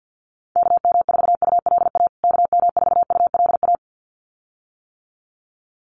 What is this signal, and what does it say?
Morse code, heard as a keyed tone, about 49 wpm, 700 Hz. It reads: KM4ULA KM4ULA
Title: USA Call Signs Repeated 2x 50wpm